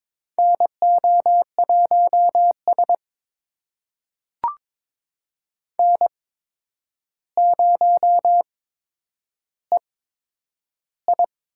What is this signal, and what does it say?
Morse code, heard as a keyed tone, about 22 wpm, spaced as for 9 wpm, 700 Hz. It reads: NO1S N0EI
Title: USA Call Signs - 4 Characters - 8x Character Spacing - 22wpm